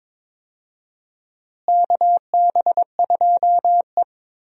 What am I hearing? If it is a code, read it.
Morse code, heard as a keyed tone, about 22 wpm, 700 Hz. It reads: KB2E
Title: USA Call Signs - 4 Characters - Repeated 3x - 22wpm